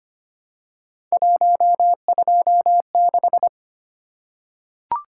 Morse code, heard as a keyed tone, about 25 wpm, 700 Hz. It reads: 126
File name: Three Letter-Number Combinations - 25wpm